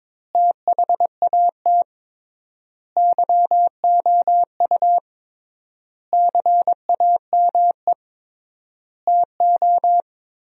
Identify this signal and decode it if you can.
Morse code, heard as a keyed tone, about 22 wpm, 700 Hz. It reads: THAT YOU CAME TO
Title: 3x Word-Spacing - Sentences from Top 200 Words - 22wpm